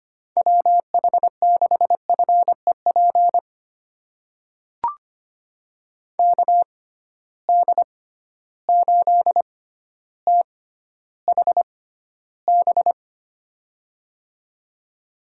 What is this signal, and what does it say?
Morse code, heard as a keyed tone, about 25 wpm, spaced as for 13 wpm, 700 Hz. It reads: WH6FEP KD8THB
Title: USA Call Signs - 6 Characters - 6x Character Spacing - 25wpm